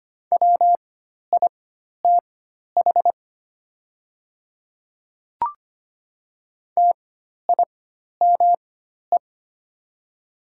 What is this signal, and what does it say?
Morse code, heard as a keyed tone, about 25 wpm, spaced as for 12 wpm, 700 Hz. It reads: WITH TIME
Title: 4x Character Spacing - Top 200 Words - 25wpm